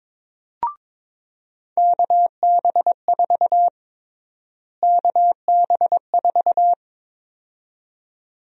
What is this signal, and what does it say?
Morse code, heard as a keyed tone, about 22 wpm, 700 Hz. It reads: KB4 KB4
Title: USA Call Sign Prefixes - Repeated 2x - 22wpm